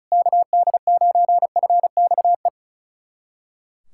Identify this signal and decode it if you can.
Morse code, heard as a keyed tone, about 36 wpm, 700 Hz. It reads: KD9FXE
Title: SOTA QSO Practice 35wpm